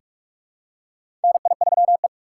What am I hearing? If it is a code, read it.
Morse code, heard as a keyed tone, about 42 wpm, 690 Hz. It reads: NI3E
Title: USA Call Signs - 6x Character Spacing - 45wpm